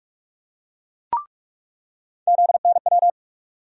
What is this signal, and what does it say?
Morse code, heard as a keyed tone, about 46 wpm, 700 Hz. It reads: ZNW
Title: USA Call Sign Postfixes - 3 Letters 45wpm